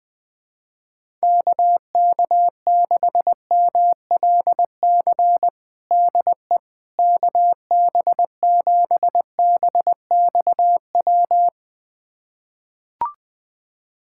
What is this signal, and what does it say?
Morse code, heard as a keyed tone, about 20 wpm, 700 Hz. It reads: KK6MLC DE KB7BXW
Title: USA Callsign DE Callsign 20wpm